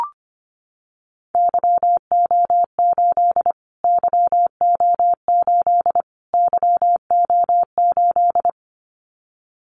Morse code, heard as a keyed tone, about 25 wpm, 700 Hz. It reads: YO8 YO8 YO8
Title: Callsign Prefixes for Top 25 DXCC - Repeated 3x - 25wpm